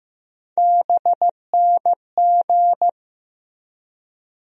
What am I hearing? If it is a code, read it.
Morse code, heard as a keyed tone, about 15 wpm, 700 Hz. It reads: BNG
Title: USA Call Sign Postfixes - 3 Letters 15wpm